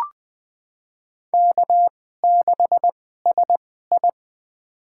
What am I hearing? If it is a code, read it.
Morse code, heard as a keyed tone, about 20 wpm, spaced as for 15 wpm, 700 Hz. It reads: K6SI
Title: USA Call Signs - 4 Characters - 2x Character Spacing - 20wpm